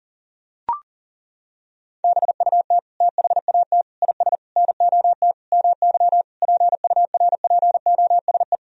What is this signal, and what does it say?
Morse code, heard as a keyed tone, about 40 wpm, 700 Hz. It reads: BUT THAT IS NOT MY PURPOSE
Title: Sentences from Top 1000 Words - Review Part 4 - 40wpm